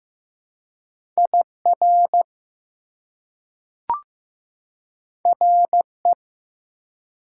Morse code, heard as a keyed tone, about 15 wpm, 700 Hz. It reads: IR RE
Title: Prefixes and Suffixes 15wpm